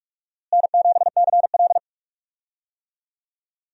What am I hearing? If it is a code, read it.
Morse code, heard as a keyed tone, about 46 wpm, 700 Hz. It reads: N7CL
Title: USA Call Signs - 4 Characters - Repeated 3x - 45wpm